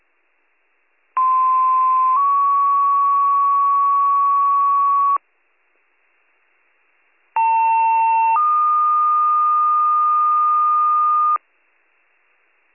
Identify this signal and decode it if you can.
Two-tone page, heard as a one-second tone, then a three-second tone, about 1040 then 1120 Hz, about 900 then 1220 Hz. It is heard from 1.2 to 11.4 s.